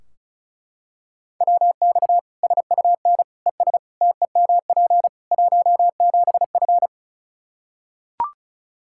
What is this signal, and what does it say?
Morse code, heard as a keyed tone, about 36 wpm, 700 Hz. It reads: WX SUN ES TEMP 17F
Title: K1USN SST QSO Practice 35wpm